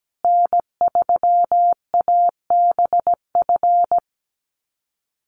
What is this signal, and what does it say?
Morse code, heard as a keyed tone, about 17 wpm, 700 Hz. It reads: N3ABF
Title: USA Call Signs Repeated 2x 17wpm